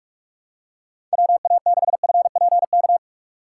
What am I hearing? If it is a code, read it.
Morse code, heard as a keyed tone, about 48 wpm, 700 Hz. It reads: WA6FPK
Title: USA Call Signs - 6 Characters - 6x Character Spacing - 45wpm